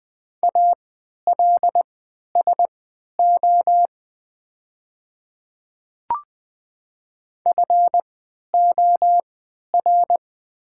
Morse code, heard as a keyed tone, about 20 wpm, spaced as for 11 wpm, 700 Hz. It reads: ALSO FOR
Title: 3x Character Spacing - Top 300 Words - 20wpm